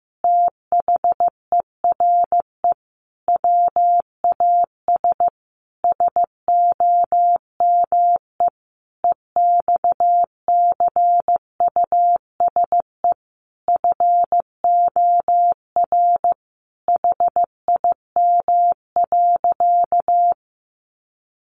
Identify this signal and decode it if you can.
Morse code, heard as a keyed tone, about 15 wpm, 700 Hz. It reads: THERE WAS SOME EXCUSE FOR HIM.
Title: Sentences from Top 700 Words 15wpm